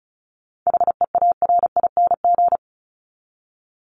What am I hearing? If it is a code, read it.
Morse code, heard as a keyed tone, about 34 wpm, 700 Hz. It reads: HEARING
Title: ING Suffix - Top 100 Words with Base Words 35wpm